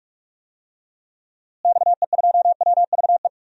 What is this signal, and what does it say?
Morse code, heard as a keyed tone, about 42 wpm, 700 Hz. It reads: XE2WVE
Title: DXCC-100 Call Signs - 6 Characters - Repeated 2x - 45wpm